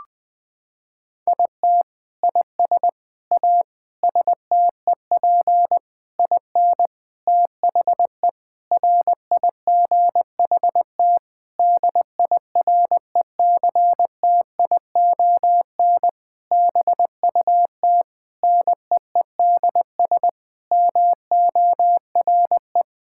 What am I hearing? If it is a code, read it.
Morse code, heard as a keyed tone, about 20 wpm, 700 Hz. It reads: IT IS A STEP IN THE RIGHT DIRECTION BUT NEEDS MORE
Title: Sentences with 8-Word Phrases - 20wpm